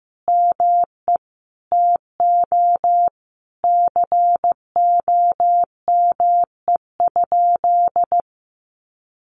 Morse code, heard as a keyed tone, about 14 wpm, 700 Hz. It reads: ME TO COME?
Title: Sentences from Top 1400 Words 15wpm